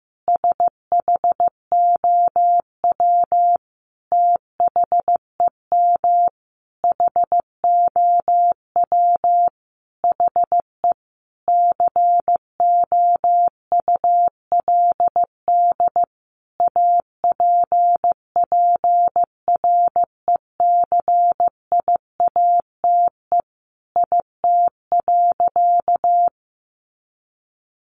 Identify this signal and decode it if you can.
Morse code, heard as a keyed tone, about 14 wpm, 700 Hz. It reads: SHOW THEM HOW HE COULD APPRECIATE IT.
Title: Sentences from Top 1300 Words 15wpm